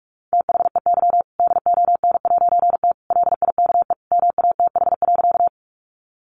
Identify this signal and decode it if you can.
Morse code, heard as a keyed tone, about 44 wpm, 700 Hz. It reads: THEY DON'T LIKE MATH.
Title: Easy Phrases 45wpm